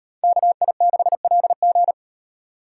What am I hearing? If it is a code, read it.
Morse code, heard as a keyed tone, about 38 wpm, 700 Hz. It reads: KI6LG